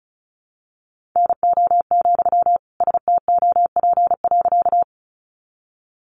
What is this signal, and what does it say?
Morse code, heard as a keyed tone, about 34 wpm, 700 Hz. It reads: NO, STOP.
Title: Sentences from Top 300 Words 35wpm